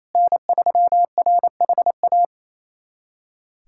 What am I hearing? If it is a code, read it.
Morse code, heard as a keyed tone, about 28 wpm, 700 Hz. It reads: N3RHA
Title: USA Callsign DE Callsign - 5 Characters - 28wpm